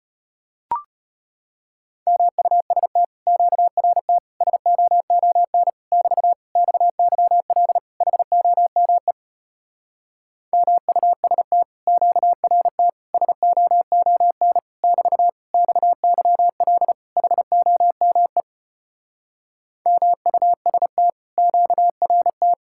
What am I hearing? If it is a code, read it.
Morse code, heard as a keyed tone, about 33 wpm, 700 Hz. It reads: MUST QRT SOON <BT> XYL HOME MUST QRT SOON <BT> XYL HOME MUST QRT